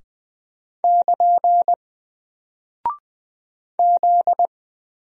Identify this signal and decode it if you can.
Morse code, heard as a keyed tone, about 20 wpm, 700 Hz. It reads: <KN> Z
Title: Single Character (Letter:Number:Punctuation:Prosign) Alt 20wpm